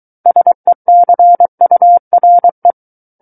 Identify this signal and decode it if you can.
Morse code, heard as a keyed tone, about 23 wpm, 700 Hz. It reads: SECURE
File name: Book One / 14-Chapter 14 - 23 WPM